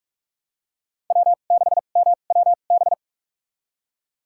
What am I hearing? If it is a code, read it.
Morse code, heard as a keyed tone, about 46 wpm, spaced as for 33 wpm, 700 Hz. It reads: W6MWB
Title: USA Call Signs - 5 Characters - 2x Character Spacing - 45wpm